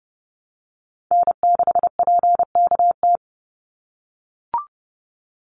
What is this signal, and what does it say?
Morse code, heard as a keyed tone, about 29 wpm, 700 Hz. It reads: N6PKT
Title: NA SOTA Chaser Callsigns 30wpm